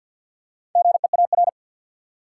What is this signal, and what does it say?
Morse code, heard as a keyed tone, about 47 wpm, 700 Hz. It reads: GEAR
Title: Four Letter Words - 50wpm